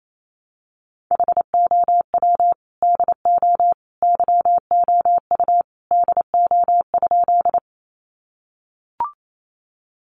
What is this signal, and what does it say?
Morse code, heard as a keyed tone, about 28 wpm, 700 Hz. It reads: HOW DO YOU DO?
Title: Sentences from Top 100 Words 28wpm